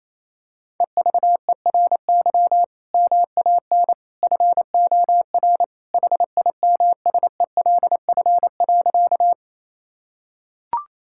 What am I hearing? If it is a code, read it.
Morse code, heard as a keyed tone, about 27 wpm, 700 Hz. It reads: EVERY MAN FOR HIMSELF.
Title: Sentences from Top 1600 Words 28wpm